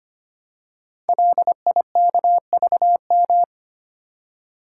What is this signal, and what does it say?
Morse code, heard as a keyed tone, about 25 wpm, 700 Hz. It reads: LIKVM
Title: Code Groups - 5 Letters - 25wpm